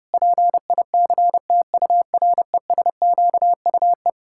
Morse code, heard as a keyed tone, about 30 wpm, 700 Hz. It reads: PICTURESQUE
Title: Pangram Challenge v2 - 30wpm